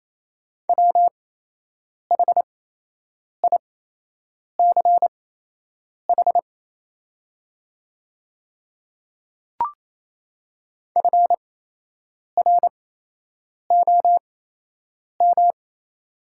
Morse code, heard as a keyed tone, about 28 wpm, spaced as for 8 wpm, 700 Hz. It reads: WHICH FROM
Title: 8x Character Spacing - Top 300 Words - 28wpm